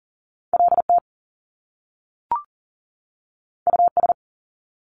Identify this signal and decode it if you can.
Morse code, heard as a keyed tone, about 40 wpm, 700 Hz. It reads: LT US
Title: Top Two Letter Combinations in Words 40wpm